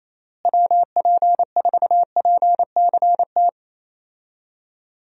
Morse code, heard as a keyed tone, about 27 wpm, 700 Hz. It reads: WP4PCT
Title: USA Call Signs - 6 Characters - Repeated 3x - 28wpm